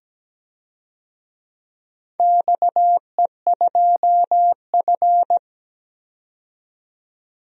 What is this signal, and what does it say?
Morse code, heard as a keyed tone, about 17 wpm, 700 Hz. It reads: XE2F
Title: DXCC-100 Call Signs - 4 Characters - 17wpm